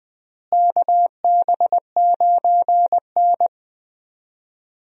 Morse code, heard as a keyed tone, about 21 wpm, 700 Hz. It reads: KB9N
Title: USA Call Signs - 4 Characters - Repeated 2x - 20wpm